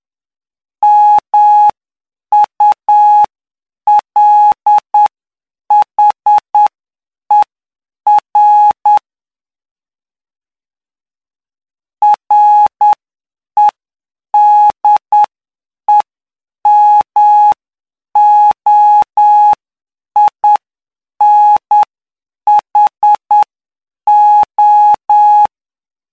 Morse code, heard as a keyed tone, about 9 wpm, spaced as for 7 wpm, 840 Hz. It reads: MULHER REDEMOINHO
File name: cw / py2lcd / 07-PY2LCD-Li��o 16